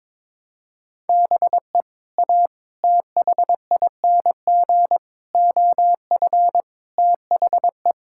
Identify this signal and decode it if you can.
Morse code, heard as a keyed tone, about 22 wpm, 700 Hz. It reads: BE A THING OF THE
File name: practice / 22wpm / Sets of 6 Words - 22wpm